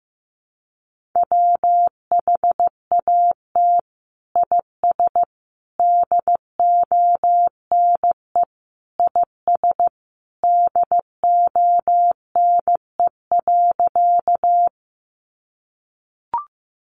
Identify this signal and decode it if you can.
Morse code, heard as a keyed tone, about 15 wpm, 700 Hz. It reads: WHAT IS DONE IS DONE.
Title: Sentences from Top 500 Words 15wpm